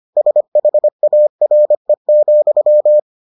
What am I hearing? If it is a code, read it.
Morse code, heard as a keyed tone, about 25 wpm, 600 Hz. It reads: SHARE,